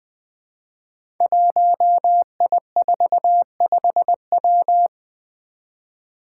Morse code, heard as a keyed tone, about 21 wpm, 700 Hz. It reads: 1I45W
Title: Five Challenging Letter-Number Combinations 20wpm